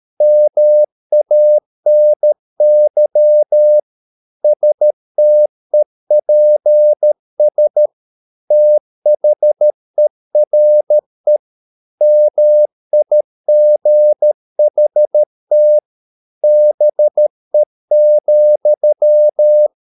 Morse code, heard as a keyed tone, about 13 wpm, 600 Hz. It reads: MANY STEPS THERE MIGHT BE,